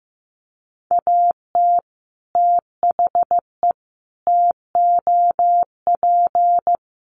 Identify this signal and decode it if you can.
Morse code, heard as a keyed tone, about 15 wpm, 700 Hz. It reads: AT THE TOP
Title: Sets of 5 Words Encore 15wpm